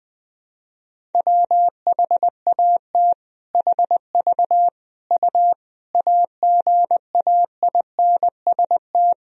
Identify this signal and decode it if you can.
Morse code, heard as a keyed tone, about 21 wpm, 700 Hz. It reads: WHAT HV U AGAINST
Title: Sentences from Top 400 Words and Abbreviations 20wpm